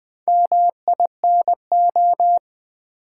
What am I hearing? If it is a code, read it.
Morse code, heard as a keyed tone, about 20 wpm, 700 Hz. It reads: MINO
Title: Build A Word - 2 Letters At A Time - 8-Letter Words - 20wpm